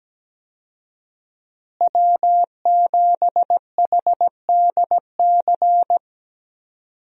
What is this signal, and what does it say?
Morse code, heard as a keyed tone, about 17 wpm, 700 Hz. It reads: W7HDC
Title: USA Call Signs - 5 Characters - Repeated 2x - 17wpm